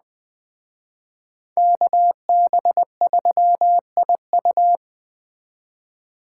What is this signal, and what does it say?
Morse code, heard as a keyed tone, about 19 wpm, 700 Hz. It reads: KB3IU